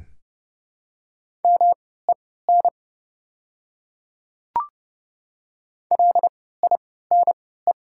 Morse code, heard as a keyed tone, about 30 wpm, spaced as for 17 wpm, 700 Hz. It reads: MEN LINE
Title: 3x Character Spacing - Top 200 Words - 30wpm